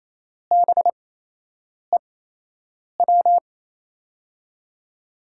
Morse code, heard as a keyed tone, about 28 wpm, spaced as for 8 wpm, 700 Hz. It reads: BEW
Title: USA Call Signs - 5 Characters - 8x Character Spacing - 28wpm